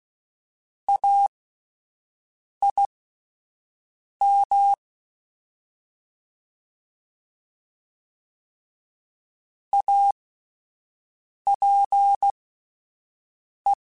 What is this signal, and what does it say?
Morse code, heard as a keyed tone, about 16 wpm, spaced as for 5 wpm, 800 Hz. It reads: AIM APE
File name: Alphabetical / Alphabetical mp3s / As @5